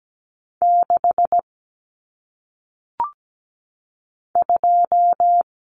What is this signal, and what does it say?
Morse code, heard as a keyed tone, about 17 wpm, 700 Hz. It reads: 6 2
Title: Numbers - 1-Digit 17wpm